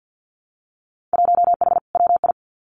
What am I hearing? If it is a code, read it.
Morse code, heard as a keyed tone, about 50 wpm, 700 Hz. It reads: 1H WI
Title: Winter Field Day Exchanges 50wpm